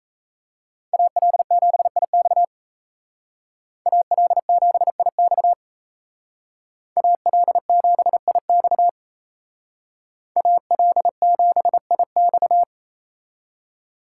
Morse code, heard as a keyed tone, about 34 wpm, 700 Hz. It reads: AL7IX AL7IX AL7IX AL7IX
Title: Speed Racing - USA Call Signs 28wpm